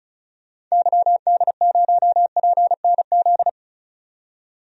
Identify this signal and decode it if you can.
Morse code, heard as a keyed tone, about 34 wpm, 700 Hz. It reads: YD0PNZ
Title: DXCC-100 Call Signs - 6 Characters - 35wpm